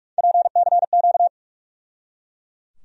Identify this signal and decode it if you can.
Morse code, heard as a keyed tone, about 42 wpm, 700 Hz. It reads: PCQ
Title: USA Call Sign Postfixes - 3 Letters 45wpm